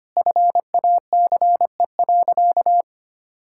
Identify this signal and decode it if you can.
Morse code, heard as a keyed tone, about 25 wpm, 700 Hz. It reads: FACE.
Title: Sentences from Top 1600 Words 25wpm